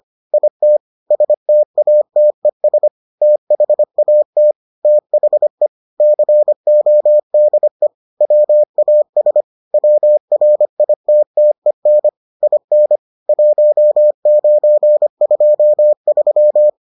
Morse code, heard as a keyed tone, about 25 wpm, 600 Hz. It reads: IT STATES THAT THE CODE WAS WRITTEN IN 1923